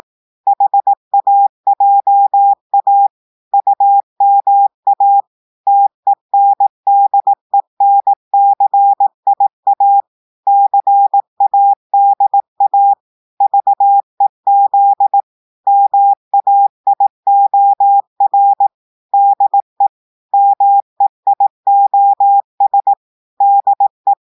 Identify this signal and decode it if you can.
Morse code, heard as a keyed tone, about 18 wpm, 800 Hz. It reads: HAJA UMA TENDENCIA CADA VEZ MAIOR DE MEIOS DE
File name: portugais texte en clair 2 vitesse 18 mots minutes